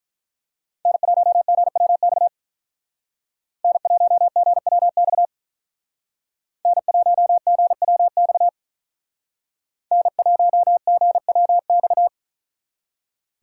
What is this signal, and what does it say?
Morse code, heard as a keyed tone, about 44 wpm, 700 Hz. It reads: N1GWX N1GWX N1GWX N1GWX